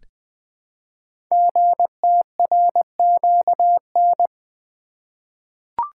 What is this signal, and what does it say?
Morse code, heard as a keyed tone, about 20 wpm, 700 Hz. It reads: GTRQN
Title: Code Groups - 5 Letters - 20wpm